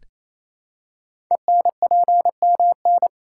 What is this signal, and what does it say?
Morse code, heard as a keyed tone, about 27 wpm, 700 Hz. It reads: ENPMN
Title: Code Groups - 5 Letter:Numbers - 28wpm